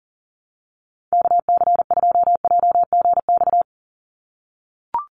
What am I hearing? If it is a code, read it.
Morse code, heard as a keyed tone, about 40 wpm, 700 Hz. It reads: KC2JGX
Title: USA Call Signs 40wpm